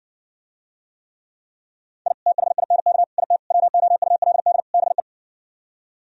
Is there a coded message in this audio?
Yes: Morse code from 2.1 to 5.0 s, heard as a keyed tone, about 60 wpm, 700 Hz. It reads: I THINK IT WOULD BE